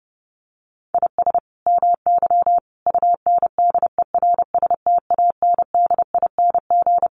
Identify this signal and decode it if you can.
Morse code, heard as a keyed tone, about 30 wpm, 700 Hz. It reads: IS MY UNDERSTANDING
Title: Sentences from Top 1400 Words 30wpm